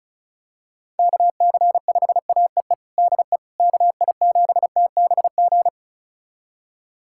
Morse code, heard as a keyed tone, about 36 wpm, 700 Hz. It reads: KC5AEE DE KI7TBG
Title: USA Callsign DE Callsign - 6 Characters - 35wpm